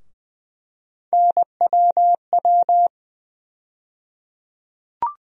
Morse code, heard as a keyed tone, about 20 wpm, 700 Hz. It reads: NWW
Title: USA Call Sign Postfixes - 3 Letters 20wpm